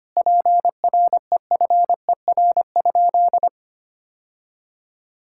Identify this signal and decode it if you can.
Morse code, heard as a keyed tone, about 25 wpm, 700 Hz. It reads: PREFER?
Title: Sentences from Top 1600 Words 25wpm